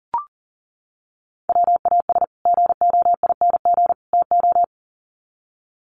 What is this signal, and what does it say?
Morse code, heard as a keyed tone, about 40 wpm, 700 Hz. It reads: WAS GOING TO
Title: Sets of 3 Words 40wpm